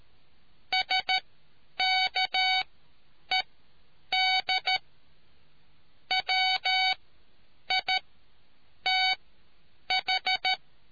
Morse code, heard as a keyed tone, about 13 wpm, spaced as for 8 wpm, 750 Hz. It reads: SKED WITH